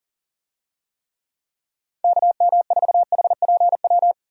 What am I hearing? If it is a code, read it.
Morse code, heard as a keyed tone, about 40 wpm, 700 Hz. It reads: KM4HPW
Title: USA Call Signs - 6 Characters - 40wpm